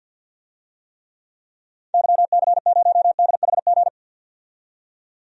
Morse code, heard as a keyed tone, about 51 wpm, 700 Hz. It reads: YC0DHG
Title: DXCC-100 Call Signs - 6 Characters - 50wpm